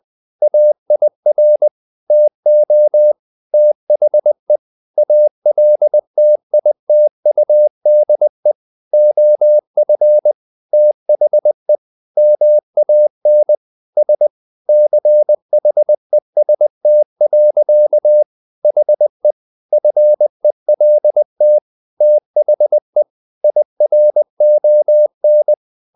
Morse code, heard as a keyed tone, about 20 wpm, 600 Hz. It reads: AIR TO THE ALTITUDE OF THE MAN S CHEST. HE FELT THE IRON
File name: the old man and the sea part 2 20wpm